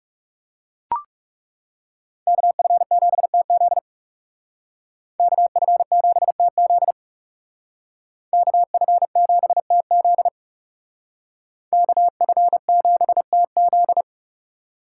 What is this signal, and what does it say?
Morse code, heard as a keyed tone, about 38 wpm, 700 Hz. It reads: KF7TZ KF7TZ KF7TZ KF7TZ